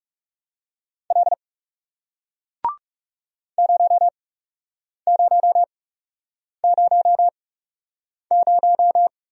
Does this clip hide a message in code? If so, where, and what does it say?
Yes: Morse code from 1.1 to 9.1 s, heard as a keyed tone, about 40 wpm, spaced as for 8 wpm, 700 Hz. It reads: L 0000